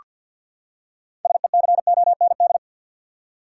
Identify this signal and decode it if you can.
Morse code, heard as a keyed tone, about 47 wpm, 700 Hz. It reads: SECOND
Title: Six Letter Words - 50wpm